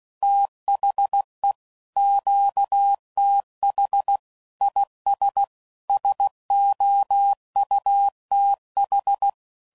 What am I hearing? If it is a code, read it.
Morse code, heard as a keyed tone, about 16 wpm, 800 Hz. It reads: THE QTH IS SOUTH
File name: CW Code Groups @15WPM / QSOs / kc7sac